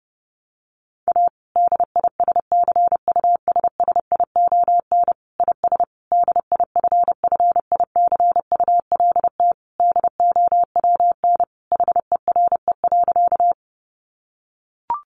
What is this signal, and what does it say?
Morse code, heard as a keyed tone, about 30 wpm, 700 Hz. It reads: A DISCUSSION IS DIFFICULT DOWN HERE.
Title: Sentences from Top 1100 Words 30wpm